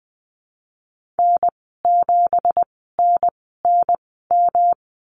Morse code, heard as a keyed tone, about 20 wpm, spaced as for 15 wpm, 700 Hz. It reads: N7NNM
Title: 167 Lesson 27g - Number 7 - Call Signs 20wpm